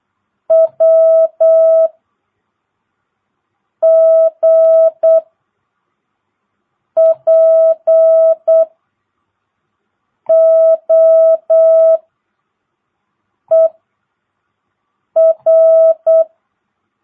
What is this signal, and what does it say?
Morse code, heard as a keyed tone, about 8 wpm, 660 Hz. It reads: W G P O E R
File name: jh8cbh / 2022cw